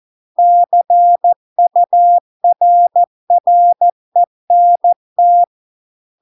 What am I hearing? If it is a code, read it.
Morse code, heard as a keyed tone, about 14 wpm, 700 Hz. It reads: CURRENT